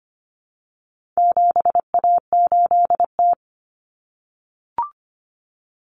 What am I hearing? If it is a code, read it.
Morse code, heard as a keyed tone, about 25 wpm, 700 Hz. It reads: 7A8T
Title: Common Cut Numbers - 4-Digit 25wpm